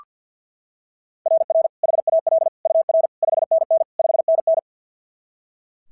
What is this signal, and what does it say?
Morse code, heard as a keyed tone, about 46 wpm, 660 Hz. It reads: RR HAL UR 5NN 5NN
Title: SOTA QSO Practice 50wpm